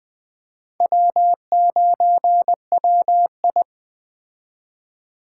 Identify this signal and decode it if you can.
Morse code, heard as a keyed tone, about 21 wpm, 700 Hz. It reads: W9WI